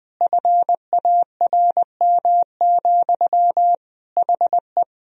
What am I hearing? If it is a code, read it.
Morse code, heard as a keyed tone, about 20 wpm, 700 Hz. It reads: FARM, HE
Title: Sentences from Top 1000 Words - Review Part 2 - 20wpm